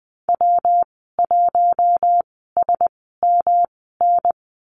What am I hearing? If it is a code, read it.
Morse code, heard as a keyed tone, about 20 wpm, spaced as for 15 wpm, 700 Hz. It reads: W1SMN
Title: 128 Lesson 21g - Number 6 - Call Signs 20wpm